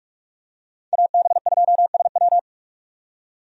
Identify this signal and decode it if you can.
Morse code, heard as a keyed tone, about 45 wpm, 700 Hz. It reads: AB2SW
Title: USA Call Signs - 5 Characters - 45wpm